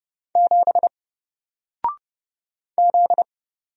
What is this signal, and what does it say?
Morse code, heard as a keyed tone, about 30 wpm, 700 Hz. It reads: 7 Z
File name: Single-Letter-Number-30wpm